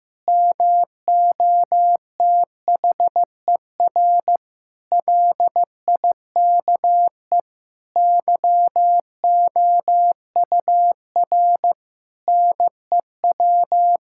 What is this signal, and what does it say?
Morse code, heard as a keyed tone, about 15 wpm, 700 Hz. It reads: MOTHER LIKE YOUR NEW
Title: Sentences from Top 1000 Words - Review Part 5 - 15wpm